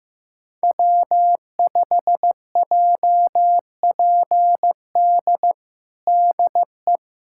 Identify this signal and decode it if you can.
Morse code, heard as a keyed tone, about 15 wpm, 700 Hz. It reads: W5JPD DE
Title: USA Callsign DE Callsign - 5 Characters - 15wpm